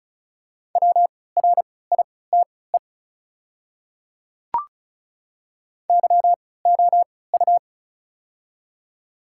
Morse code, heard as a keyed tone, about 36 wpm, spaced as for 20 wpm, 690 Hz. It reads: WRITE YOU
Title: 3x Character Spacing - Top 300 Words - 35wpm